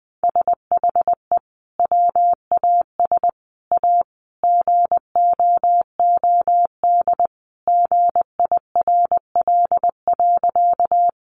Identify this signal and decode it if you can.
Morse code, heard as a keyed tone, about 20 wpm, 700 Hz. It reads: SHE WAS A GOOD GIRL.
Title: Sentences from Top 400 Words 20wpm